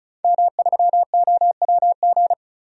Morse code, heard as a keyed tone, about 34 wpm, 700 Hz. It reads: M3OWG
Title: DXCC-100 Call Signs - 5 Characters - 35wpm